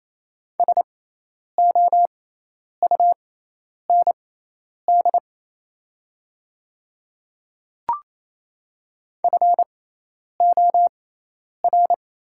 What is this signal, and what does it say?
Morse code, heard as a keyed tone, about 28 wpm, spaced as for 10 wpm, 700 Hz. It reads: SOUND FOR
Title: 6x Character Spacing - Top 100 Words - 28wpm